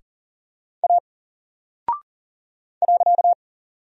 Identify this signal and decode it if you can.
Morse code, heard as a keyed tone, about 40 wpm, 700 Hz. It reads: A .
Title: Single Character (Letter:Number:Punctuation:Prosign) Alt 40wpm